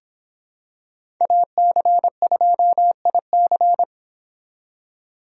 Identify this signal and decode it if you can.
Morse code, heard as a keyed tone, about 26 wpm, 700 Hz. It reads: AC2IC